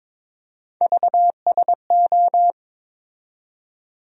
Morse code, heard as a keyed tone, about 21 wpm, 700 Hz. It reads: VSO
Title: USA Call Sign Postfixes - 3 Letters 22wpm